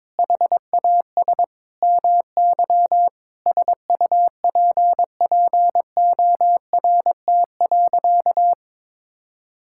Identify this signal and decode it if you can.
Morse code, heard as a keyed tone, about 22 wpm, 700 Hz. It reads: HAS MY SUPPORT.
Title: Sentences from Top 1000 Words - Review Part 5 - 22wpm